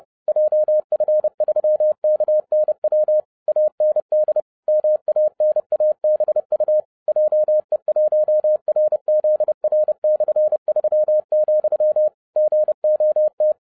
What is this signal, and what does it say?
Morse code, heard as a keyed tone, about 30 wpm, 610 Hz. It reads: JF3KNW AND MANABU JE1RZR/3, GOT